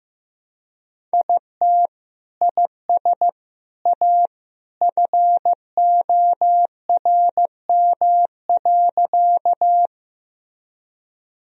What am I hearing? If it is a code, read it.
Morse code, heard as a keyed tone, about 15 wpm, 700 Hz. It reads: IT IS A FORM.
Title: Sentences from Top 1000 Words - Review Part 1 - 15wpm